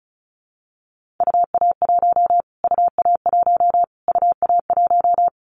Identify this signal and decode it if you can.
Morse code, heard as a keyed tone, about 34 wpm, 700 Hz. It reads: UA1 UA1 UA1
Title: Callsign Prefixes for Top 25 DXCC - Repeated 3x - 35wpm